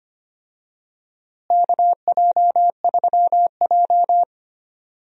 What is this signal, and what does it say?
Morse code, heard as a keyed tone, about 25 wpm, 700 Hz. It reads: KJ3J
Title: USA Call Signs - 4 Characters - Repeated 3x - 25wpm